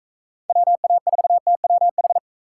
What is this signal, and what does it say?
Morse code, heard as a keyed tone, about 38 wpm, 700 Hz. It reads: WA4TWH